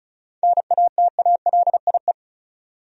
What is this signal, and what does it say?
Morse code, heard as a keyed tone, about 34 wpm, 700 Hz. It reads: NATALIE
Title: USA Common Names - 35wpm